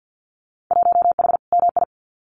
Morse code, heard as a keyed tone, about 50 wpm, 700 Hz. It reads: 1H MI
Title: Winter Field Day Exchanges 50wpm